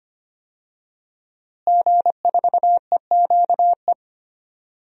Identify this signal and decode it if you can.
Morse code, heard as a keyed tone, about 25 wpm, 700 Hz. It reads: G4EQE
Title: DXCC-100 Call Signs - 5 Characters - Repeated 2x - 25wpm